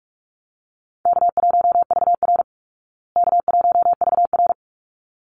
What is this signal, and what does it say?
Morse code, heard as a keyed tone, about 44 wpm, 700 Hz. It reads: K1VR K1VR
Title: Easy Call Signs Repeated 2x - 45wpm